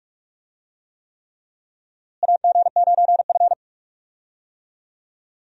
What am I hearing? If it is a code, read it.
Morse code, heard as a keyed tone, about 46 wpm, 700 Hz. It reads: AG9F
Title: USA Call Signs - 4 Characters - Repeated 3x - 45wpm